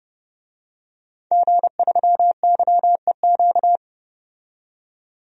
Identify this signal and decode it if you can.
Morse code, heard as a keyed tone, about 30 wpm, 700 Hz. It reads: G3YEQ
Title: DXCC-100 Call Signs - 5 Characters - 30wpm